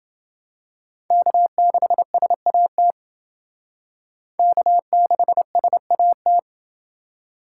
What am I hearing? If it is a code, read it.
Morse code, heard as a keyed tone, about 28 wpm, 700 Hz. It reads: K6SAT K6SAT